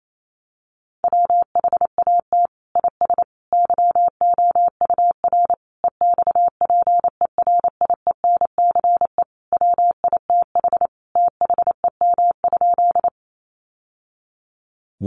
Morse code, heard as a keyed tone, about 28 wpm, 700 Hz. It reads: WHAT IS YOUR EXPERIENCE WITH THEM?
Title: Sentences from Top 600 Words 28wpm